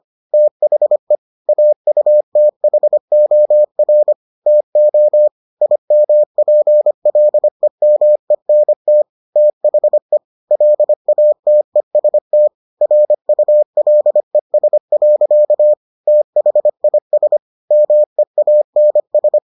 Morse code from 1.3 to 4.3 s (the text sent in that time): AUTHOR